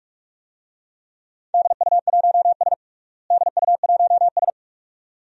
Morse code, heard as a keyed tone, about 46 wpm, 700 Hz. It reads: DU1S DU1S
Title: DXCC-100 Call Signs - 4 Characters - Repeated 2x - 45wpm